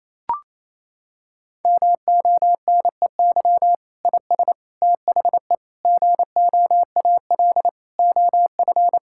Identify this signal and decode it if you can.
Morse code, heard as a keyed tone, about 28 wpm, 700 Hz. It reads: MONEY IS THE GOAL OF
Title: Sentences from Top 1000 Words - Review Part 5 - 28wpm